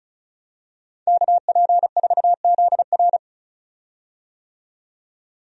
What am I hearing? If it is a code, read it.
Morse code, heard as a keyed tone, about 36 wpm, 700 Hz. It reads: KP4ZR
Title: USA Call Signs - 5 Characters - 35wpm